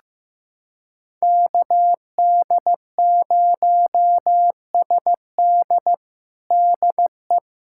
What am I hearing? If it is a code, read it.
Morse code, heard as a keyed tone, about 15 wpm, 700 Hz. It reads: KD0SD DE
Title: USA Callsign DE Callsign 15wpm